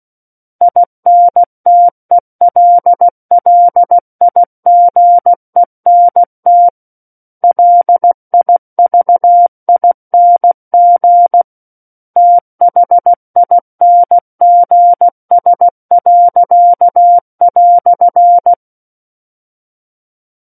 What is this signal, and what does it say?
Morse code, heard as a keyed tone, about 16 wpm, 700 Hz. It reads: INTELLIGENT LIVING THINGS."
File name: Book One / 07-Chapter 7 - 16 WPM